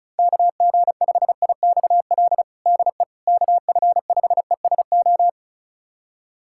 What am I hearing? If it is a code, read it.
Morse code, heard as a keyed tone, about 36 wpm, 700 Hz. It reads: KG5IXL DE KF5ESO
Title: USA Callsign DE Callsign - 6 Characters - 35wpm